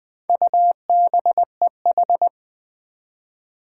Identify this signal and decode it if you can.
Morse code, heard as a keyed tone, about 20 wpm, 700 Hz. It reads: UBEH
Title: Four Challenging Letter-Number Combinations 20wpm